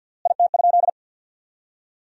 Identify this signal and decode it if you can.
Morse code, heard as a keyed tone, about 50 wpm, 700 Hz. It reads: IT?
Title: 3x Word-Spacing - Sentences from Top 100 Words - 50wpm